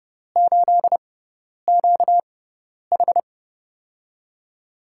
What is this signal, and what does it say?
Morse code, heard as a keyed tone, about 31 wpm, spaced as for 10 wpm, 700 Hz. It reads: 8QH
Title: USA Call Signs - 4 Characters - 6x Character Spacing - 30wpm